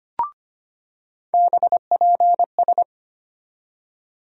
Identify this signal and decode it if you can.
Morse code, heard as a keyed tone, about 25 wpm, 700 Hz. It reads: BPS
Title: USA Call Sign Postfixes - 3 Letters 25wpm